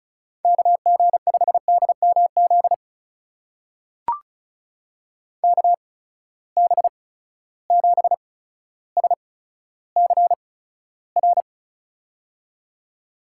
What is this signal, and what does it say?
Morse code, heard as a keyed tone, about 34 wpm, spaced as for 15 wpm, 700 Hz. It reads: KG5DMZ KB7SCR
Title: USA Call Signs - 6 Characters - 8x Character Spacing - 35wpm